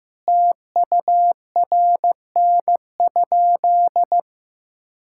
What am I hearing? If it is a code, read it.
Morse code, heard as a keyed tone, about 15 wpm, 700 Hz. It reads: TURN?
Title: Sentences from Top 1600 Words 15wpm